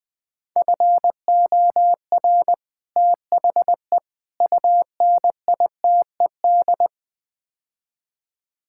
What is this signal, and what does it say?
Morse code, heard as a keyed tone, about 20 wpm, 700 Hz. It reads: FOR THE UNITED
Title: Build A Phrase - 4-Word Phrases - 20wpm